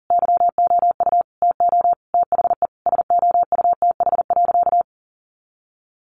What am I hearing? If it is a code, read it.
Morse code, heard as a keyed tone, about 40 wpm, 700 Hz. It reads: YOU TO THE SOUTH.
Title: Sentences from Top 400 Words 40wpm